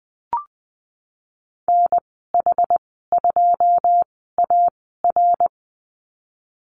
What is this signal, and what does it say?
Morse code, heard as a keyed tone, about 20 wpm, spaced as for 15 wpm, 700 Hz. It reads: NH2AR
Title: 088 Lesson 15g - Number 5 - Call Signs 20wpm